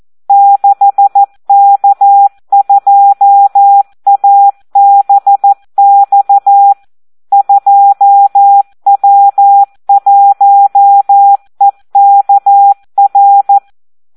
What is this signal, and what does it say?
Morse code, heard as a keyed tone, about 14 wpm, 800 Hz. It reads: 6K2ABX 2W1EKR
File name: Audio practica CW / SD 1 14